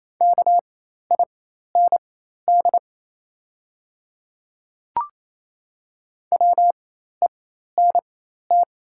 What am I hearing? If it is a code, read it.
Morse code, heard as a keyed tone, about 28 wpm, spaced as for 13 wpm, 700 Hz. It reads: KIND WENT
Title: 4x Character Spacing - Top 200 Words - 28wpm